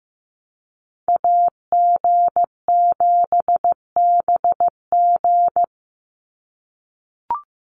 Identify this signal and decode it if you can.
Morse code, heard as a keyed tone, about 15 wpm, 700 Hz. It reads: AG7BG
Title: USA Call Signs 15wpm